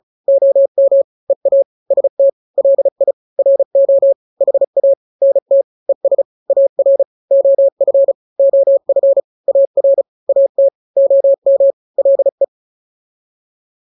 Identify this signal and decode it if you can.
Morse code, heard as a keyed tone, about 34 wpm, 550 Hz. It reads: OM EA ST LI RO HA NT ES AR OF OF AR AT OM LE
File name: n-grams / 2-grams 35 30 WPM